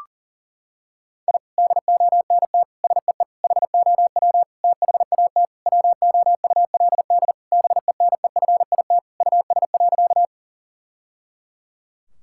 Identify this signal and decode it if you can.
Morse code, heard as a keyed tone, about 40 wpm, 700 Hz. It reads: I DONT SEE HOW THAT WOULD BENEFIT US.
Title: Sentences from Top 1000 Words - Review Part 6 - 40wpm